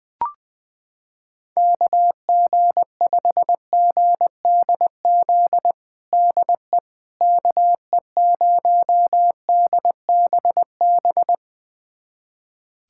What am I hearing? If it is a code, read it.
Morse code, heard as a keyed tone, about 20 wpm, 700 Hz. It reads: KG5GDZ DE KE0DBB
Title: USA Callsign DE Callsign - 6 Characters - 20wpm